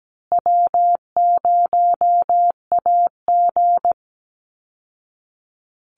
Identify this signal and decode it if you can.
Morse code, heard as a keyed tone, about 17 wpm, 700 Hz. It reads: W0AG
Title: Call Signs Repeated 2x - 17wpm